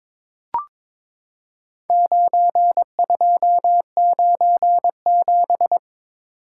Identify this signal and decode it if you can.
Morse code, heard as a keyed tone, about 22 wpm, 700 Hz. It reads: 9297
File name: practice / 22wpm / Four Letter-Number Combinations - 22wpm